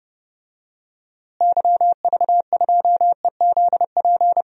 Morse code, heard as a keyed tone, about 31 wpm, 700 Hz. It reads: YV2EZP
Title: DXCC-100 Call Signs - 6 Characters - Repeated 3x - 30wpm